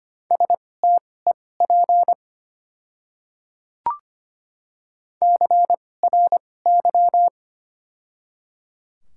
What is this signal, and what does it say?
Morse code, heard as a keyed tone, about 25 wpm, spaced as for 18 wpm, 700 Hz. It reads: STEP CRY
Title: 2x Character Spacing - Top 500 Words - 25wpm